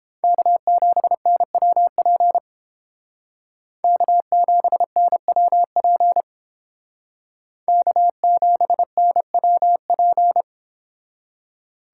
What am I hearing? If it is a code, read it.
Morse code, heard as a keyed tone, about 31 wpm, 700 Hz. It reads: K7NWP K7NWP K7NWP